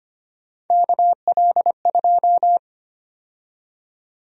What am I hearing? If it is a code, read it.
Morse code, heard as a keyed tone, about 25 wpm, 700 Hz. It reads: KL2
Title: USA Call Sign Prefixes - 25wpm